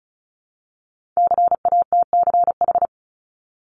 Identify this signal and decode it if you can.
Morse code, heard as a keyed tone, about 36 wpm, 700 Hz. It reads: CATCH
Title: Multi-letter Phonemes 35wpm